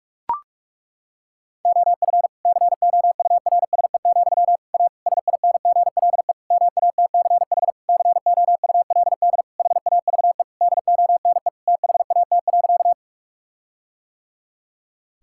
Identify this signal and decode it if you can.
Morse code, heard as a keyed tone, about 42 wpm, 700 Hz. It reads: OF COURSE, A SINGLE MATCH COULD HAVE DONE THAT.
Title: Sentences from Top 1600 Words 45wpm